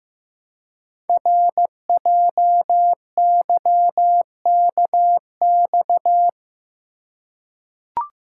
Morse code, heard as a keyed tone, about 15 wpm, 700 Hz. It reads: RJYKX
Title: Code Groups - 5 Letters - 15wpm